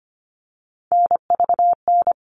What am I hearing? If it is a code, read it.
Morse code, heard as a keyed tone, about 25 wpm, 700 Hz. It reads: NVN
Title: All Cut Numbers - 3-Digit 25wpm